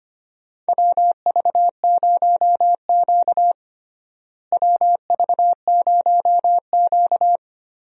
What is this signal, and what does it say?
Morse code, heard as a keyed tone, about 25 wpm, 700 Hz. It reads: WV0Q WV0Q
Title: USA Call Signs - 4 Characters - Repeated 3x - 25wpm